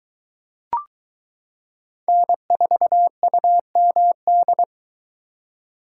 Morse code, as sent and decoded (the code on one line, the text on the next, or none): -. ....- ..- -- -..
N4UMD